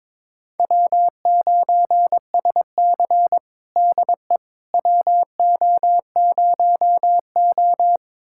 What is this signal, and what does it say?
Morse code, heard as a keyed tone, about 21 wpm, 700 Hz. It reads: W9SC DE WO0O
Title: USA Callsign DE Callsign - 4 Characters - 22wpm